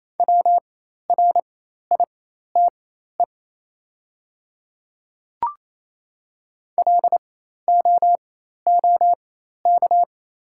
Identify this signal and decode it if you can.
Morse code, heard as a keyed tone, about 27 wpm, spaced as for 13 wpm, 700 Hz. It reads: WRITE LOOK
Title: 4x Character Spacing - Top 100 Words - 28wpm